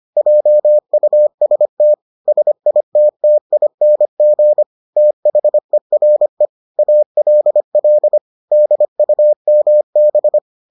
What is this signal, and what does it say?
Morse code, heard as a keyed tone, about 25 wpm, 600 Hz. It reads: JUST SITTING THERE ALL DUMB